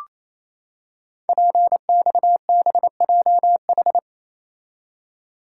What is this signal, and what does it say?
Morse code, heard as a keyed tone, about 27 wpm, 700 Hz. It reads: PXBJH
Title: Code Groups - 5 Letters - 28wpm